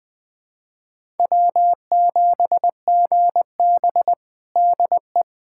Morse code, heard as a keyed tone, about 21 wpm, 700 Hz. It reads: W7GB DE
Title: Practice QSO - Exchange 4 - Ending - 20wpm